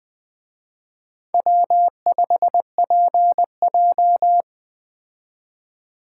Morse code, heard as a keyed tone, about 21 wpm, 700 Hz. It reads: W5PJ